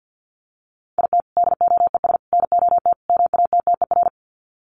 Morse code, heard as a keyed tone, about 48 wpm, 700 Hz. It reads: IT DOES NOT MATTER
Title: Sets of 4 Words 50wpm